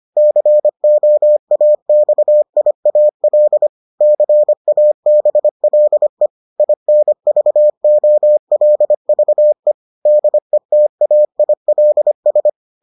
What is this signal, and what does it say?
Morse code, heard as a keyed tone, about 25 wpm, 600 Hz. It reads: COAXIAL CABLE INVOLVE DETAILS